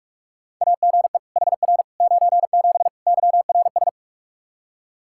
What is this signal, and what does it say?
Morse code, heard as a keyed tone, about 46 wpm, 700 Hz. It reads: AGE HR 97 YRS
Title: Practice QSO - Exchange 3 - More Info - 45wpm